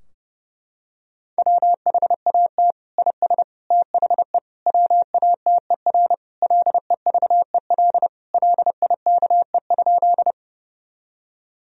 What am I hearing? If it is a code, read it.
Morse code, heard as a keyed tone, about 30 wpm, 700 Hz. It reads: WHAT IS THE WATER LEVEL LIKE?
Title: Sentences from Top 1000 Words - Review Part 4 - 30wpm